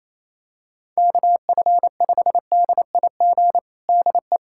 Morse code, heard as a keyed tone, about 28 wpm, 700 Hz. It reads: KF5DIG DE
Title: USA Callsign DE Callsign 28wpm